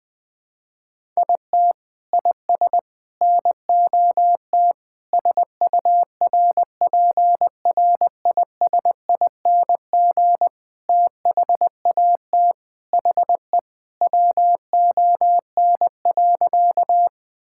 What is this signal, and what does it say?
Morse code, heard as a keyed tone, about 21 wpm, 700 Hz. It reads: IT IS NOT SURPRISING THAT HE WON.
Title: Sentences with 5-Word Phrases - 20wpm